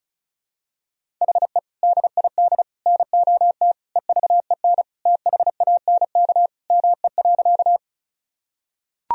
Morse code, heard as a keyed tone, about 36 wpm, 700 Hz. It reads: HE DID NOT EVEN THANK ME.
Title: Sentences from Top 1000 Words - Review Part 1 - 35wpm